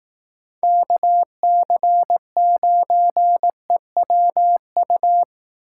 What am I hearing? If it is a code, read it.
Morse code, heard as a keyed tone, about 18 wpm, 700 Hz. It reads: KC9EWU
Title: Speed Racing - USA Call Signs 15wpm